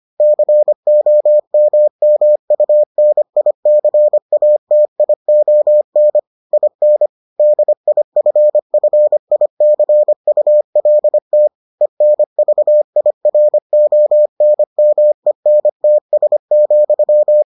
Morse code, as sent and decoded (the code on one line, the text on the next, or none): -.-. --- -- -- ..- -. .. -.-. .- - .. --- -. / .. -. / -.. .. ..-. ..-. .. -.-. ..- .-.. - / . -. ...- .. .-. --- -. -- . -. - ... --..--
COMMUNICATION IN DIFFICULT ENVIRONMENTS,